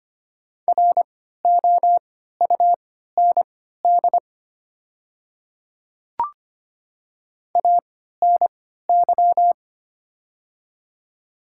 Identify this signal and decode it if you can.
Morse code, heard as a keyed tone, about 25 wpm, spaced as for 14 wpm, 700 Hz. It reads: ROUND ANY
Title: 3x Character Spacing - Top 300 Words - 25wpm